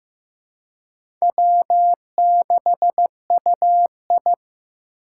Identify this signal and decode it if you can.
Morse code, heard as a keyed tone, about 18 wpm, 700 Hz. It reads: W6UI
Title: Speed Racing - USA 1x2 Call Signs 15wpm